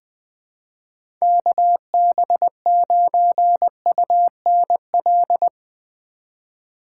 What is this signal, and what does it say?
Morse code, heard as a keyed tone, about 20 wpm, 700 Hz. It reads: KB9UNL
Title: USA Call Signs - 4x Character Spacing - 20wpm